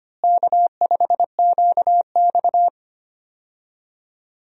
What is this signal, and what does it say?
Morse code, heard as a keyed tone, about 25 wpm, 700 Hz. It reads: K5QX
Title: USA Call Signs - 4 Characters - Repeated 3x - 25wpm